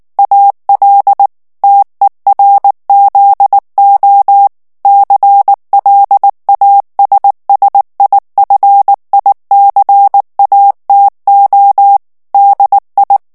Morse code, heard as a keyed tone, about 19 wpm, 800 Hz. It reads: AL TERZO CLASSIFICATO DI
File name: morse mp3 / ITALIEN texte clair 2 vitesse 18 mots minute